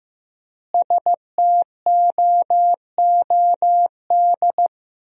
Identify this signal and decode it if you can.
Morse code, heard as a keyed tone, about 15 wpm, 700 Hz. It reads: STOOD
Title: 4x Word-Spacing - Sentences from Top 500 Words - 15wpm